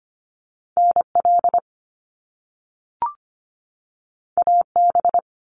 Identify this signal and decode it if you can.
Morse code, heard as a keyed tone, about 25 wpm, 700 Hz. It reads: NL AB
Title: Canadian Provinces and Territories 25wpm